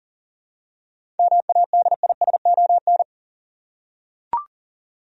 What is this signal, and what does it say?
Morse code, heard as a keyed tone, about 40 wpm, 700 Hz. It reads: MADISON
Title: USA Common Names - 40wpm